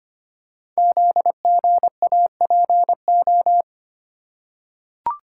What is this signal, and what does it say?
Morse code, heard as a keyed tone, about 25 wpm, 700 Hz. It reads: ZGAPO
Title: Code Groups - 5 Letters - 25wpm